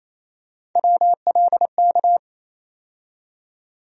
Morse code, heard as a keyed tone, about 28 wpm, 700 Hz. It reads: WLK
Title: USA Call Sign Postfixes - 3 Letters 28wpm